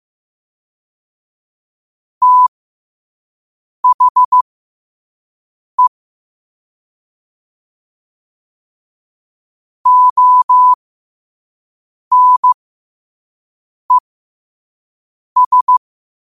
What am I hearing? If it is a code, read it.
Morse code, heard as a keyed tone, about 15 wpm, spaced as for 5 wpm, 1000 Hz. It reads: THE ONES